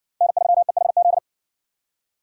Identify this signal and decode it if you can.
Morse code, heard as a keyed tone, about 55 wpm, 700 Hz. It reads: N3EHZ